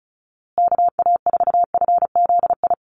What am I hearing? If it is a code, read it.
Morse code, heard as a keyed tone, about 34 wpm, 700 Hz. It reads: KA4FZI
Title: USA Call Signs Repeated 2x 35wpm